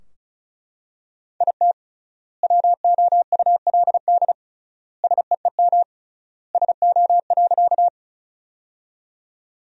Morse code, heard as a keyed tone, about 36 wpm, 700 Hz. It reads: IT WOULD SEEM SO.
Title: 3x Word-Spacing - Sentences from Top 300 Words - 35wpm